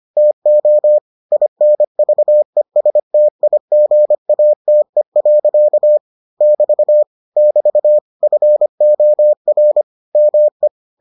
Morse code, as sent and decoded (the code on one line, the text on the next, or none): - --- / .. -. ...- . ... - .. --. .- - . .-.-.- / -...- / -...- / ..-. --- .-. / -- .
TO INVESTIGATE. <BT> <BT> FOR ME